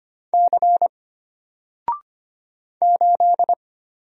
Morse code, heard as a keyed tone, about 25 wpm, 700 Hz. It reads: C 8
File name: Single-Letter-Number-25wpm